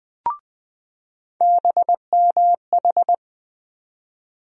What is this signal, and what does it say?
Morse code, heard as a keyed tone, about 20 wpm, 700 Hz. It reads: BMH
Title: USA Call Sign Postfixes - 3 Letters 20wpm